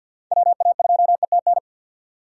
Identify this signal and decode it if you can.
Morse code, heard as a keyed tone, about 64 wpm, 700 Hz. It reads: WA2ETN